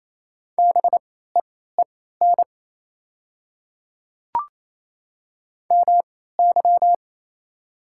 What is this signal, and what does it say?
Morse code, heard as a keyed tone, about 28 wpm, spaced as for 16 wpm, 700 Hz. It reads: BEEN MY
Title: 3x Character Spacing - Top 100 Words - 28wpm